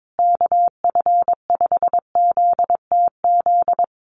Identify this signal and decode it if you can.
Morse code, heard as a keyed tone, about 22 wpm, 700 Hz. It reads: KF5ZTZ
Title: USA Call Signs Repeated 3x 22wpm